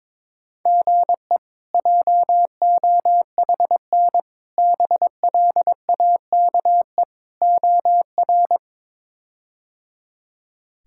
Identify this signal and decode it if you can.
Morse code, heard as a keyed tone, about 22 wpm, 700 Hz. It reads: GE JOHN BLAKE OR
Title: K1USN SST QSO Practice - 22wpm